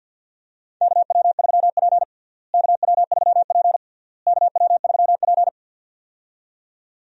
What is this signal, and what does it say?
Morse code, heard as a keyed tone, about 52 wpm, 700 Hz. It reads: KW3P KW3P KW3P
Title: USA Call Signs - 4 Characters - Repeated 3x - 50wpm